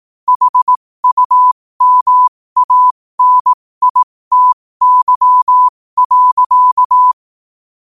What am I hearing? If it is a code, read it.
Morse code, heard as a keyed tone, about 18 wpm, spaced as for 15 wpm, 1000 Hz. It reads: HUMANITY.